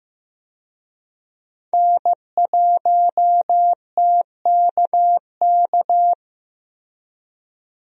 Morse code, heard as a keyed tone, about 15 wpm, 700 Hz. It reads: N1TKK
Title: USA Call Signs - 5 Characters - Repeated 3x - 15wpm